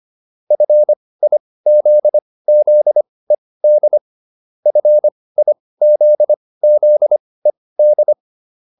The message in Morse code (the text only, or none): FIZZED FIZZED